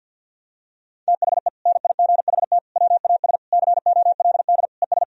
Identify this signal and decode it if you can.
Morse code, heard as a keyed tone, about 49 wpm, 700 Hz. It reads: THE NIGHT WAS COLD ES
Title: Sentences from Top 400 Words and Abbreviations 50wpm